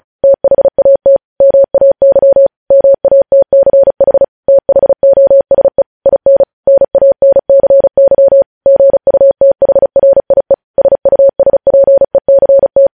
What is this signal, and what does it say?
Morse code, heard as a keyed tone, about 36 wpm, 550 Hz. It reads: THAT MAY MATCH THOSE IN NANCY GUTHRIE SUSPECT